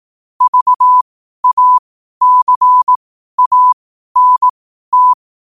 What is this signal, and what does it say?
Morse code, heard as a keyed tone, about 18 wpm, spaced as for 12 wpm, 1000 Hz. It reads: VACANT
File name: qotdfiles / 12wpm